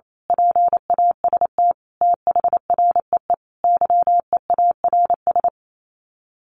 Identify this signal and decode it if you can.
Morse code, heard as a keyed tone, about 28 wpm, 700 Hz. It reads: PAST THREE YEARS
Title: Sets of 5 Words Encore2 28wpm